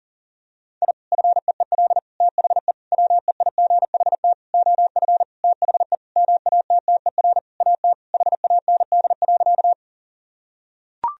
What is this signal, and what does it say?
Morse code, heard as a keyed tone, about 40 wpm, 700 Hz. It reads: I FEEL THE WEIGHT OF THE MATTER AT HAND.
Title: Sentences from Top 1000 Words - Review Part 3 - 40wpm